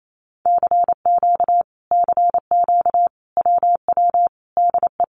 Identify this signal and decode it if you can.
Morse code, heard as a keyed tone, about 28 wpm, 700 Hz. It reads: CQ CQ WW DE
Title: Common QSO Phrases Encore 28wpm